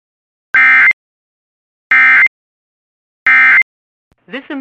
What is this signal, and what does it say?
Emergency Alert System (SAME) tones, heard as harsh buzzing bursts, with speech. It is heard from 0.5 to 3.6 s.